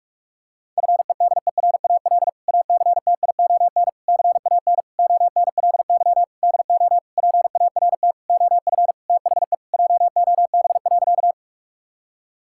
Morse code, heard as a keyed tone, about 46 wpm, 700 Hz. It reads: FEDERAL ACTION CAN ONLY DO PART OF THE JOB.
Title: Sentences from Top 1000 Words - Review Part 5 - 45wpm